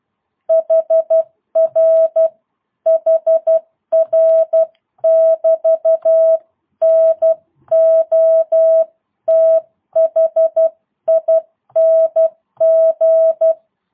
Morse code, heard as a keyed tone, about 12 wpm, 650 Hz. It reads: HR HR <BT> NOTHING
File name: jh8cbh / 2022cw